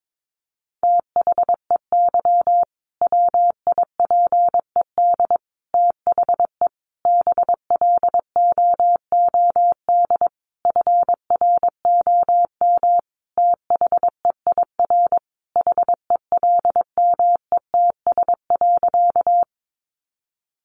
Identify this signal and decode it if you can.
Morse code, heard as a keyed tone, about 22 wpm, 700 Hz. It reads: THEY WIPED THE BLOOD FROM THEIR HELMETS.
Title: A Sound Of Thunder 22wpm